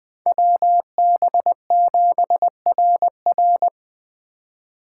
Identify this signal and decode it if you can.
Morse code, heard as a keyed tone, about 20 wpm, 700 Hz. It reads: WB7RR
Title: Speed Racing - USA 2x2 Call Signs 17wpm